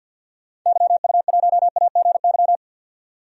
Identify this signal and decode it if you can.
Morse code, heard as a keyed tone, about 52 wpm, 700 Hz. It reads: YU1AGY
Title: DXCC-100 Call Signs - 6 Characters - 50wpm